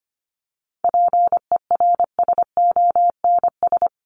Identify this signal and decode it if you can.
Morse code, heard as a keyed tone, about 25 wpm, 700 Hz. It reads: PERSONS
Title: ES:S Suffix - Top 100 Words with Base Words 25wpm